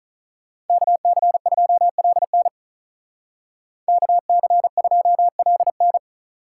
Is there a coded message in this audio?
Yes: Morse code from 0.7 to 6.0 s, heard as a keyed tone, about 46 wpm, 700 Hz. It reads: KC2LN KC2LN